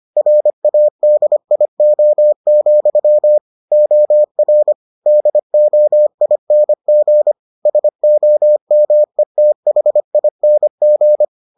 Morse code, heard as a keyed tone, about 25 wpm, 600 Hz. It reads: RADIO, OR DOING SOMETHING